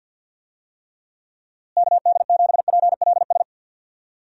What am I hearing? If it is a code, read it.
Morse code, heard as a keyed tone, about 51 wpm, 700 Hz. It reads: KD7PLS
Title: USA Call Signs - 6 Characters - Repeated 2x - 50wpm